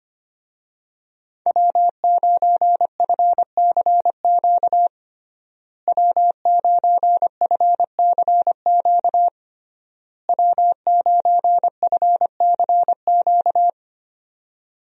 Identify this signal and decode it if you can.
Morse code, heard as a keyed tone, about 25 wpm, 700 Hz. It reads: W9FCQ W9FCQ W9FCQ
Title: USA Call Signs - 5 Characters - Repeated 3x - 25wpm